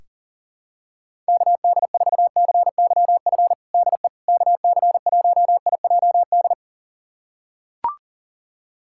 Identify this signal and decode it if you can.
Morse code, heard as a keyed tone, about 40 wpm, 700 Hz. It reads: KD4CYF DE KC1IJD
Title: USA Callsign DE Callsign - 6 Characters - 40wpm